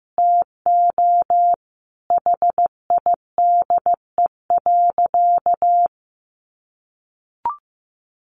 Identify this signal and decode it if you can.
Morse code, heard as a keyed tone, about 15 wpm, 700 Hz. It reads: TO HIDE.
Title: Easy Phrases 15wpm